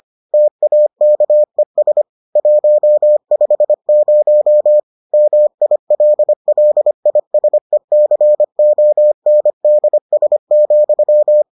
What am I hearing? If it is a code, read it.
Morse code, heard as a keyed tone, about 25 wpm, 600 Hz. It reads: TAKES 150 MILLISECONDS,